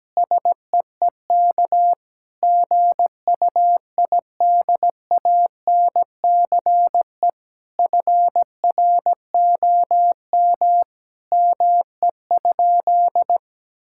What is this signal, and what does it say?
Morse code, heard as a keyed tone, about 17 wpm, 700 Hz. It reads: SEEK GUIDANCE FROM ME?
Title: Sentences from Top 1600 Words 17wpm